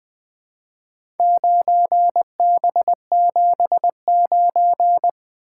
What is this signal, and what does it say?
Morse code, heard as a keyed tone, about 20 wpm, 700 Hz. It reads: 9B79
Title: Four Letter-Number Combinations - 20wpm